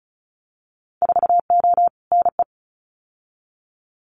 Morse code, heard as a keyed tone, about 36 wpm, 700 Hz. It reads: 4O NE
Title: Winter Field Day Exchanges Encore 35wpm